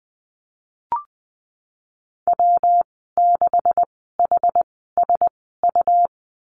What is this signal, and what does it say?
Morse code, heard as a keyed tone, about 20 wpm, spaced as for 15 wpm, 700 Hz. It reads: W6HSU
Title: 161 Lesson 26g - Letter g - Call Signs 20wpm